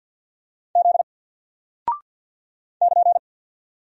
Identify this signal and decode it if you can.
Morse code, heard as a keyed tone, about 48 wpm, 700 Hz. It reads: Z <KN>
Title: Single Character (Letter:Number:Punctuation:Prosign) Alt 50wpm